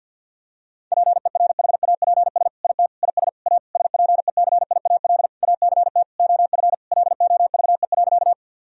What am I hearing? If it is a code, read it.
Morse code, heard as a keyed tone, about 52 wpm, 700 Hz. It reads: PERHAPS IT IS A SPECIAL ACT OF LOVE.
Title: Sentences from Top 1000 Words - Review Part 3 - 50wpm